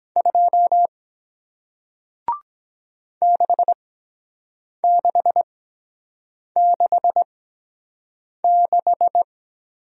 Morse code, heard as a keyed tone, about 23 wpm, spaced as for 7 wpm, 700 Hz. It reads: 2 6666